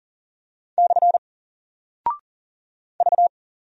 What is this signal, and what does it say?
Morse code, heard as a keyed tone, about 40 wpm, 700 Hz. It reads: / V
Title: Single Character (Letter:Number:Punctuation:Prosign) Alt 40wpm